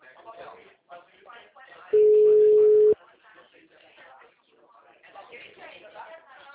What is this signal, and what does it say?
Ringback tone (Europe, 425 Hz), with speech.